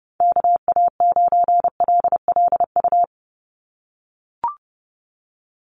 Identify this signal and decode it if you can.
Morse code, heard as a keyed tone, about 30 wpm, 700 Hz. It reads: KA9LLU
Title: USA Call Signs 30wpm